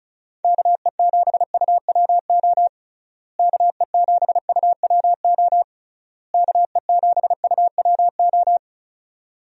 Morse code, heard as a keyed tone, about 36 wpm, 700 Hz. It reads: KE7UWO KE7UWO KE7UWO
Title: USA Call Signs - 6 Characters - Repeated 3x - 35wpm